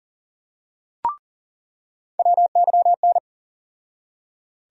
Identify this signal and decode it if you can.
Morse code, heard as a keyed tone, about 40 wpm, 700 Hz. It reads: WYN
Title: USA Call Sign Postfixes - 3 Letters 40wpm